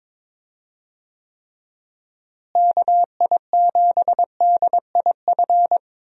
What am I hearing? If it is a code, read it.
Morse code, heard as a keyed tone, about 22 wpm, 700 Hz. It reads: KI7DIF
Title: USA Call Signs - 6 Characters - Repeated 3x - 22wpm